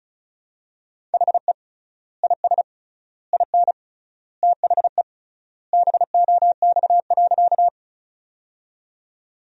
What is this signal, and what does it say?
Morse code, heard as a keyed tone, about 36 wpm, 700 Hz. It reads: HE IS IN THE BOX.
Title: 3x Word-Spacing - Sentences from Top 500 Words - 35wpm